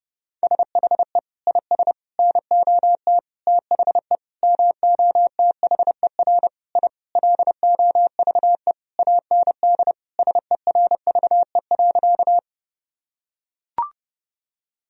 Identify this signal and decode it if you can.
Morse code, heard as a keyed tone, about 30 wpm, 700 Hz. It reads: SHE IS NOT THE MOTHER I LOVE AND SERVE.
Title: Sentences from Top 1000 Words - Review Part 2 - 30wpm